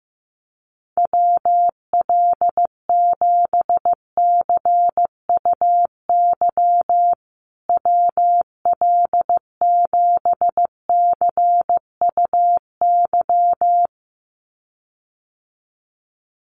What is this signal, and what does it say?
Morse code, heard as a keyed tone, about 15 wpm, 700 Hz. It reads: WL7CUY WL7CUY
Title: USA Call Signs Repeated 2x 15wpm